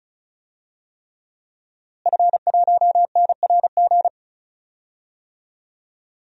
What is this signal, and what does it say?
Morse code, heard as a keyed tone, about 34 wpm, 700 Hz. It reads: F1NRG
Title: DXCC-100 Call Signs - 5 Characters - Repeated 3x - 35wpm